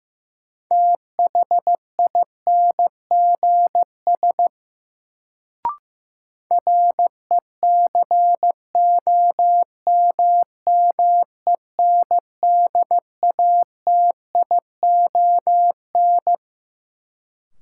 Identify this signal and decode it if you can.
Morse code, heard as a keyed tone, about 15 wpm, 700 Hz. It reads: THINGS RECOMMENDATION
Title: Sentences from Top 1400 Words - New Word Drill Alt - 15wpm